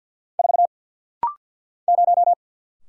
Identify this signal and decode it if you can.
Morse code, heard as a keyed tone, about 48 wpm, 700 Hz. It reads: 4 0
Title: Single Letter-Number - Rapid-Fire 50wpm